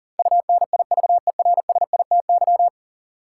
Morse code, heard as a keyed tone, about 40 wpm, 700 Hz. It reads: UNIVERSITY
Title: Sentences from Top 1000 Words - Review Part 4 - New Word Drill - 40wpm